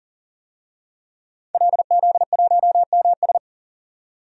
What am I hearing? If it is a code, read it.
Morse code, heard as a keyed tone, about 40 wpm, 700 Hz. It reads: LZ1MS
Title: DXCC-100 Call Signs - 5 Characters - 40wpm